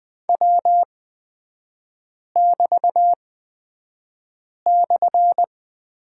Morse code, heard as a keyed tone, about 20 wpm, spaced as for 5 wpm, 700 Hz. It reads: W<BT>/
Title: 250 Lesson 40e - BK - Three Characters 20wpm